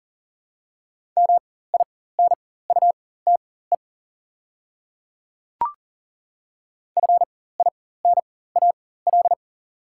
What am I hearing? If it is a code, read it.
Morse code, heard as a keyed tone, about 40 wpm, spaced as for 19 wpm, 700 Hz. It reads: MINUTE FINAL
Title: 4x Character Spacing - Top 500 Words - 40wpm